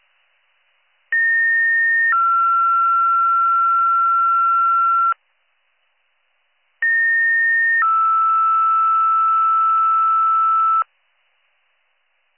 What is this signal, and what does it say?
Two-tone page, heard as a one-second tone, then a three-second tone, about 1850 then 1360 Hz, about 1850 then 1320 Hz. It is heard from 1.1 to 10.8 s.